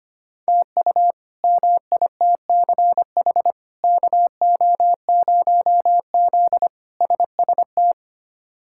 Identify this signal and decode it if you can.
Morse code, heard as a keyed tone, about 25 wpm, 700 Hz. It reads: TU MITCH KO0Z SST
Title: K1USN SST QSO Practice - 25wpm